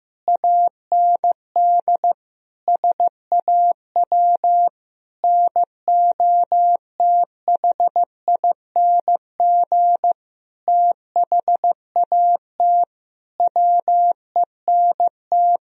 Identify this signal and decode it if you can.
Morse code, heard as a keyed tone, about 15 wpm, 700 Hz. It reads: AND SAW NOTHING THAT WENT
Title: Sentences from Top 1000 Words - Review Part 3 - 15wpm